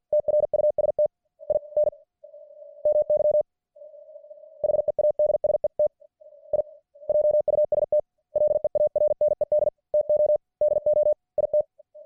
Morse code, heard as a keyed tone, about 48 wpm, 600 Hz. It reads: TRUST IN MY HEADSET I JUST LEARNED TO DO IT